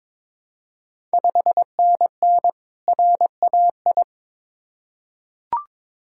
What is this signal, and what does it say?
Morse code, heard as a keyed tone, about 22 wpm, 700 Hz. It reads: 5NN RAI
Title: Ontario QSO Party Exchange 22wpm